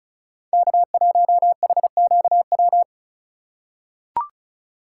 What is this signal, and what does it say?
Morse code, heard as a keyed tone, about 34 wpm, 700 Hz. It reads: K1HQW
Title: USA Call Signs - 5 Characters - 2x Character Spacing - 35wpm